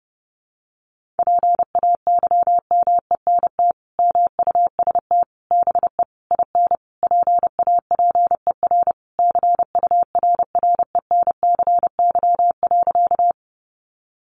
Morse code, heard as a keyed tone, about 30 wpm, 700 Hz. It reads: PAYMENT MUST BE IN PAPER CURRENCY.
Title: Sentences from Top 1500 Words 30wpm